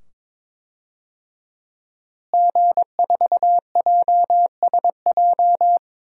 Morse code, heard as a keyed tone, about 22 wpm, 700 Hz. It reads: G4JSJ
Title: DXCC-100 Call Signs - 5 Characters - Repeated 2x - 22wpm